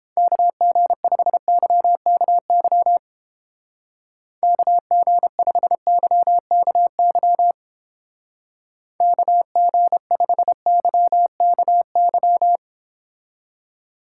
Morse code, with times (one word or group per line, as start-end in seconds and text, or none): 0.2-3.0 KG5YKY
4.4-7.5 KG5YKY
9.0-12.6 KG5YKY